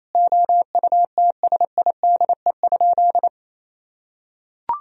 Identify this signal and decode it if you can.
Morse code, heard as a keyed tone, about 28 wpm, 700 Hz. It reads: OUTSIDE?
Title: Sentences from Top 1000 Words - Review Part 4 - 28wpm